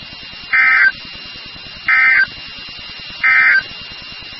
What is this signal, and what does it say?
Emergency Alert System (SAME) tones, heard as harsh buzzing bursts. It is heard from 0.4 to 3.7 s.
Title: NOAA Weather Radio WXK36